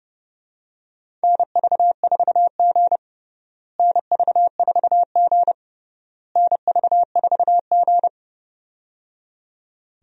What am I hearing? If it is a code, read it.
Morse code, heard as a keyed tone, about 30 wpm, 700 Hz. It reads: NV4G NV4G NV4G
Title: USA Call Signs - 4 Characters - Repeated 3x - 30wpm